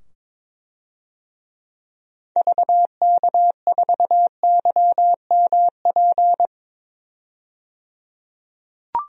Morse code, heard as a keyed tone, about 22 wpm, 700 Hz. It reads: VK4YMP
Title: DXCC-100 Call Signs - 6 Characters - 22wpm